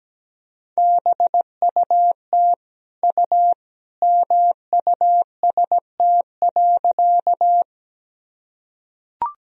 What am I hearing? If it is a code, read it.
Morse code, heard as a keyed tone, about 17 wpm, 700 Hz. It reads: BUT U MUST.
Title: Sentences from Top 200 Words and Abbreviations 17wpm